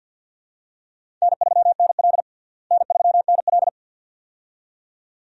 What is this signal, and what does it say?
Morse code, heard as a keyed tone, about 52 wpm, 700 Hz. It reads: N3NL N3NL
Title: USA Call Signs - 4 Characters - Repeated 2x - 50wpm